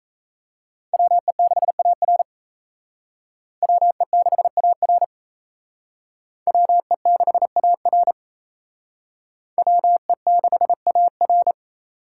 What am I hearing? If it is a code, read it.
Morse code, heard as a keyed tone, about 35 wpm, 700 Hz. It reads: WE6AR WE6AR WE6AR WE6AR
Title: Speed Racing - USA 2x2 Call Signs 28wpm